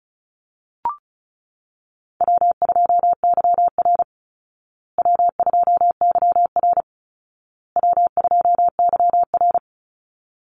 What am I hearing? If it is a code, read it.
Morse code, heard as a keyed tone, about 34 wpm, 700 Hz. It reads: W2YR W2YR W2YR
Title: Call Signs Repeated 2x - 35wpm